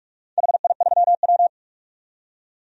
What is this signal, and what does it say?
Morse code, heard as a keyed tone, about 44 wpm, 700 Hz. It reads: HI3W
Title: DXCC-100 Call Signs - 4 Characters - 45wpm